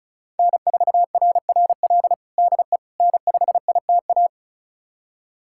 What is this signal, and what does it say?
Morse code, heard as a keyed tone, about 34 wpm, 700 Hz. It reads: N4RRL DE N5ITA
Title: USA Callsign DE Callsign - 5 Characters - 35wpm